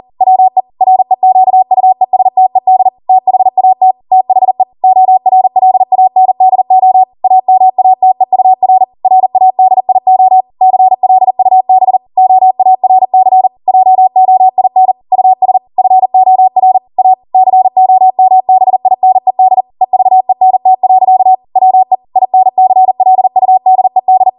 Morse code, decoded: WE REQUESTED THAT THE ORLANDO AMATEUR RADIO CLUB OARC JOIN US FOR A COMBINED EVENT. WE INCLUDED